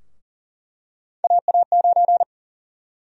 Morse code, heard as a keyed tone, about 40 wpm, 700 Hz. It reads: AA9
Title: USA Call Sign Prefixes - 40wpm